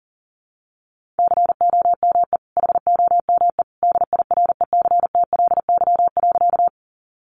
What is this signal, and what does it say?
Morse code, heard as a keyed tone, about 40 wpm, 700 Hz. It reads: COME HOME DIRECTLY.
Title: Sentences from Top 800 Words 40wpm